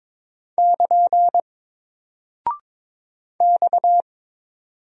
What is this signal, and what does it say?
Morse code, heard as a keyed tone, about 22 wpm, 700 Hz. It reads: <KN> X
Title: Single Character (Letter:Number:Punctuation:Prosign) Alt 22wpm